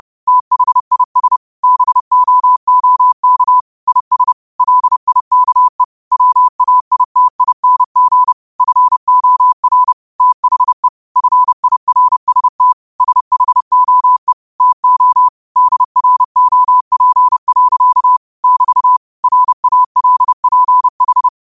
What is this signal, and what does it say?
Morse code, heard as a keyed tone, about 30 wpm, 1000 Hz. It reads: THIS BOOK IS LIKE WAITING FOR THE FIRST SHOE TO DROP. <BT> RALPH
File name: qotdfiles / 30wpm